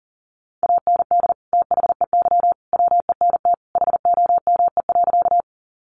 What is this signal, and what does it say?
Morse code, heard as a keyed tone, about 40 wpm, 700 Hz. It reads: AND THEY WENT HOME.
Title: Sentences from Top 200 Words 40wpm